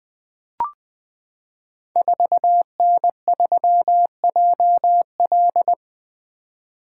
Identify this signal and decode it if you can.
Morse code, heard as a keyed tone, about 20 wpm, 700 Hz. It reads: 4N3JL
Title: Code Groups - 5 Letter:Numbers - 20wpm